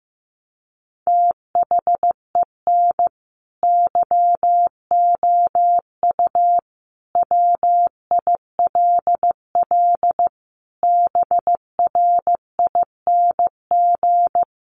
Morse code, heard as a keyed tone, about 15 wpm, 700 Hz. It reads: THEN YOU WILL BRING
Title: Sentences from Top 1500 Words 15wpm